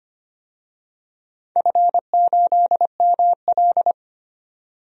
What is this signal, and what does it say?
Morse code, heard as a keyed tone, about 25 wpm, 700 Hz. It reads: F8ML
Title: DXCC-100 Call Signs - 4 Characters - 25wpm